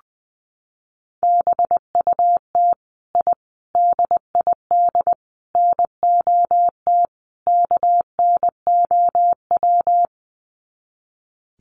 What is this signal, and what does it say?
Morse code, heard as a keyed tone, about 20 wpm, 700 Hz. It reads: BUT I DID NOT KNOW
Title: Sets of 5 Words Encore 20wpm